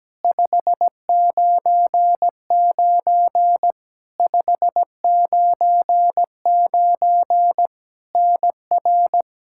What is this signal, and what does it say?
Morse code, heard as a keyed tone, about 17 wpm, 700 Hz. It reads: 599 599 NR
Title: K1USN SST QSO Practice 17wpm